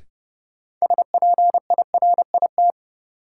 Morse code, heard as a keyed tone, about 30 wpm, 700 Hz. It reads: SPIRIT
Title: Sentences from Top 1400 Words - New Word Drill Alt - 30wpm